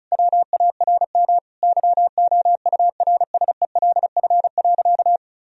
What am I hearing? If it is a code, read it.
Morse code, heard as a keyed tone, about 34 wpm, 700 Hz. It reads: WARM YOURSELF.
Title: Sentences from Top 1000 Words - Review Part 2 - 35wpm